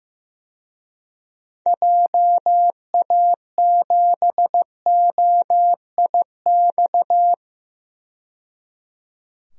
Morse code, heard as a keyed tone, about 15 wpm, 700 Hz. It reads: JA7OIX
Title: DXCC-100 Call Signs - 6 Characters - 15wpm